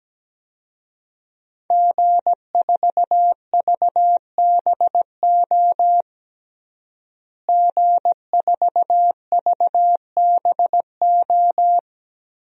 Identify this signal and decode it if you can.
Morse code, heard as a keyed tone, about 17 wpm, 700 Hz. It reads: G4VBO G4VBO
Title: DXCC-100 Call Signs - 5 Characters - Repeated 3x - 17wpm